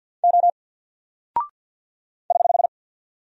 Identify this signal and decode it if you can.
Morse code, heard as a keyed tone, about 48 wpm, 700 Hz. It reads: O <HH>
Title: Single Character (Letter:Number:Punctuation:Prosign) Alt 50wpm